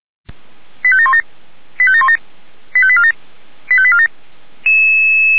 Five-tone selective call (ZVEI), heard as quick runs of five tones, tones of about 70 ms. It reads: 85717 85717 85847 85847